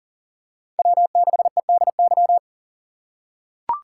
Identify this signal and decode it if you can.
Morse code, heard as a keyed tone, about 37 wpm, 700 Hz. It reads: W6EDY
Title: K1USN SST QSO Practice - 40wpm